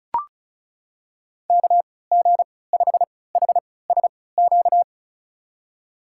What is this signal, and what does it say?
Morse code, heard as a keyed tone, about 36 wpm, spaced as for 20 wpm, 700 Hz. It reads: KG5HSQ
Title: USA Call Signs - 6 Characters - 3x Character Spacing - 35wpm